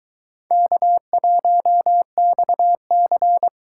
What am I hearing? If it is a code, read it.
Morse code, heard as a keyed tone, about 24 wpm, 700 Hz. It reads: K1XC